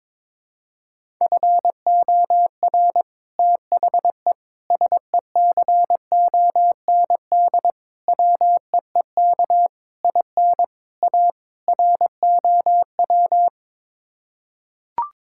Morse code, heard as a keyed tone, about 22 wpm, 700 Hz. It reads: FOR THE SECOND WEEK IN A ROW
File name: practice / 22wpm / Sets of 7 Words - 22wpm